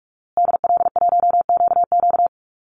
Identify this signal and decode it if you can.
Morse code, heard as a keyed tone, about 46 wpm, 700 Hz. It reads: DL1QQ
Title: Call Signs Repeated 2x - 45wpm